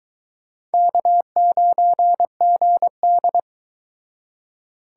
Morse code, heard as a keyed tone, about 22 wpm, 700 Hz. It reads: K9GD